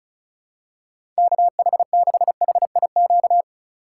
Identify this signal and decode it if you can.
Morse code, heard as a keyed tone, about 36 wpm, 700 Hz. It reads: KH6HIQ
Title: USA Call Signs - 6 Characters - 4x Character Spacing - 35wpm